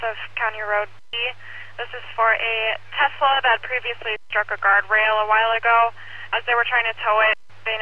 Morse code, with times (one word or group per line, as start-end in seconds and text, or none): none